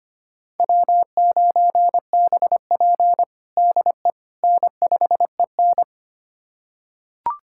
Morse code, heard as a keyed tone, about 25 wpm, 700 Hz. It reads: W9BP DE N5EN
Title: USA Callsign DE Callsign - 4 Characters - 25wpm